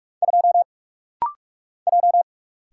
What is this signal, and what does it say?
Morse code, heard as a keyed tone, about 45 wpm, 700 Hz. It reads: 2 J
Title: Single Letter-Number Alt - Rapid-Fire 45wpm